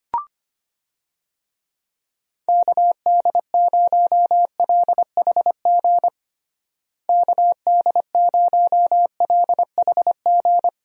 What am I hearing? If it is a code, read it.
Morse code, heard as a keyed tone, about 25 wpm, 700 Hz. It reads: KD0LHG KD0LHG
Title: USA Call Signs - 6 Characters - Repeated 2x - 25wpm